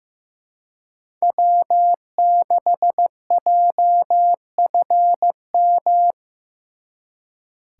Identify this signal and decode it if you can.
Morse code, heard as a keyed tone, about 18 wpm, 700 Hz. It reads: W6JFM
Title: Speed Racing - USA 1x3 Call Signs 15wpm